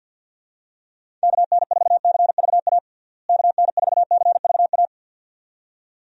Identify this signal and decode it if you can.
Morse code, heard as a keyed tone, about 53 wpm, 700 Hz. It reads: KN4CVA KN4CVA
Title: USA Call Signs - 6 Characters - Repeated 2x - 50wpm